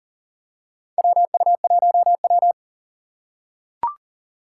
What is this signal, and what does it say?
Morse code, heard as a keyed tone, about 40 wpm, 700 Hz. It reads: WU1W
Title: USA Call Signs - 4 Characters - 3x Character Spacing - 40wpm